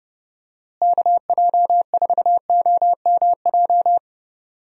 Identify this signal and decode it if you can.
Morse code, heard as a keyed tone, about 30 wpm, 700 Hz. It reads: KJ4OMJ
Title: USA Call Signs - 6 Characters - Repeated 2x - 30wpm